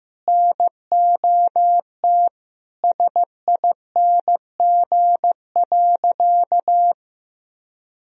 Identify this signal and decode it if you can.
Morse code, heard as a keyed tone, about 15 wpm, 700 Hz. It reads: NOT SING.
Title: Sentences from Top 1000 Words - Review Part 2 - 15wpm